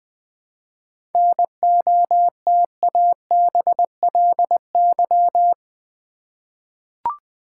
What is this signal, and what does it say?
Morse code, heard as a keyed tone, about 20 wpm, 690 Hz. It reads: NOTABLY
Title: 7 Letter Words - First 3 Letters Given - 20wpm